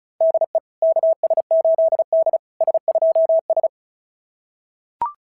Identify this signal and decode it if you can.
Morse code, heard as a keyed tone, about 36 wpm, 660 Hz. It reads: DE KS8D S2S
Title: SOTA QSO Practice 35wpm